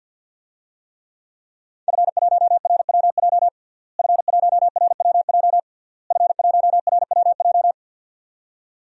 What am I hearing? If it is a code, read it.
Morse code, heard as a keyed tone, about 52 wpm, 700 Hz. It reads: F1RWJ F1RWJ F1RWJ
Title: DXCC-100 Call Signs - 5 Characters - Repeated 3x - 50wpm